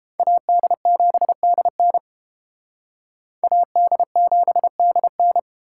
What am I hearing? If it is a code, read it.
Morse code, heard as a keyed tone, about 30 wpm, 700 Hz. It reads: AD7DN AD7DN